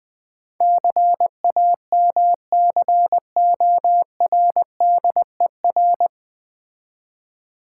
Morse code, heard as a keyed tone, about 21 wpm, 700 Hz. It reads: CAMCORDER
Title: Difficult Words - Speed-Racing with Hints - 20wpm